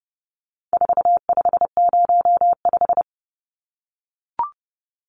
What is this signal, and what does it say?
Morse code, heard as a keyed tone, about 30 wpm, 700 Hz. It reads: 4505
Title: Numbers - 4-Digit 30wpm